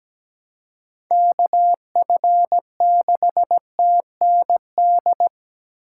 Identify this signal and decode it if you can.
Morse code, heard as a keyed tone, about 17 wpm, 700 Hz. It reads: KF6TND
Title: USA Call Signs - 6 Characters - 4x Character Spacing - 17wpm